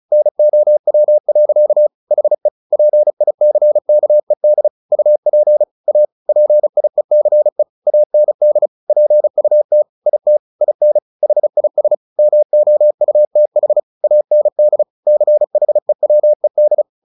Morse code, heard as a keyed tone, about 35 wpm, 600 Hz. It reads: NOW. HE PICKED UP A PIECE AND PUT IT IN HIS MOUTH AND CHEWED
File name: the old man and the sea / the old man and the sea 1 35wpm